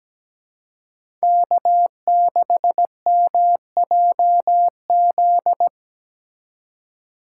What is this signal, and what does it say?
Morse code, heard as a keyed tone, about 20 wpm, 700 Hz. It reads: K6MJZ
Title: Speed Racing - USA 1x3 Call Signs 17wpm